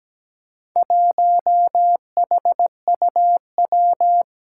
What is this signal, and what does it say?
Morse code, heard as a keyed tone, about 17 wpm, 700 Hz. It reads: 1HUW
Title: Four Challenging Letter-Number Combinations 17wpm